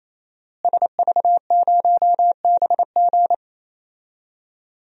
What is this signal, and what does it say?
Morse code, heard as a keyed tone, about 28 wpm, 700 Hz. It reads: SV0BG
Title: DXCC-100 Call Signs - 5 Characters - Repeated 3x - 28wpm